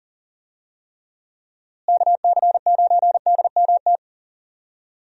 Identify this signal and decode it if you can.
Morse code, heard as a keyed tone, about 40 wpm, 700 Hz. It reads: KC9DMT
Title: USA Call Signs - 6 Characters - Repeated 2x - 40wpm